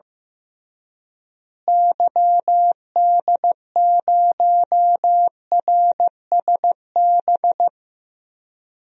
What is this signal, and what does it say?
Morse code, heard as a keyed tone, about 15 wpm, 700 Hz. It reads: YD0RSB
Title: DXCC-100 Call Signs - 6 Characters - Repeated 3x - 15wpm